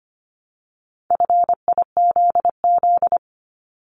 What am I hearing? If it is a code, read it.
Morse code, heard as a keyed tone, about 25 wpm, 700 Hz. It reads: FIZZ
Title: Multi-letter Phonemes 25wpm